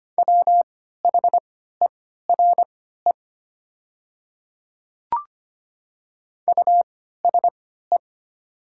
Morse code, heard as a keyed tone, about 25 wpm, spaced as for 14 wpm, 700 Hz. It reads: WHERE USE
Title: 3x Character Spacing - Top 300 Words - 25wpm